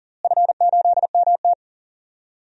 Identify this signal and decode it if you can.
Morse code, heard as a keyed tone, about 40 wpm, 700 Hz. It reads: F8MT
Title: DXCC-100 Call Signs - 4 Characters - Repeated 2x - 40wpm